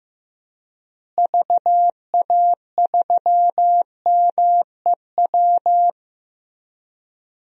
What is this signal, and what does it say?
Morse code, heard as a keyed tone, about 15 wpm, 700 Hz. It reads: VA3MEW
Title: DXCC-100 Call Signs - 6 Characters - Repeated 3x - 15wpm